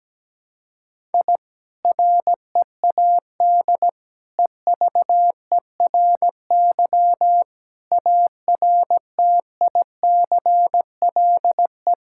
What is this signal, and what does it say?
Morse code, heard as a keyed tone, about 17 wpm, 700 Hz. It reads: I READ EVERY ARTICLE
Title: Sentences from Top 1000 Words - Review Part 5 - 17wpm